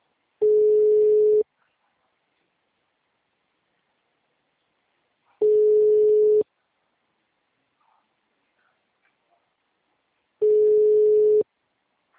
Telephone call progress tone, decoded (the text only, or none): ringback tone